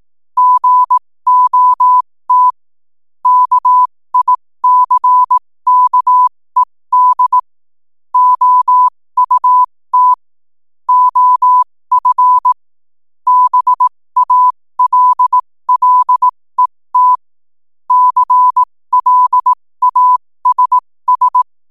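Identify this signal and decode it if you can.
Morse code, heard as a keyed tone, about 18 wpm, spaced as for 15 wpm, 1000 Hz. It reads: GOT KICKED OUT OF BALLET CLASS
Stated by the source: 15 wpm Morse code from RNZN Communicators Association.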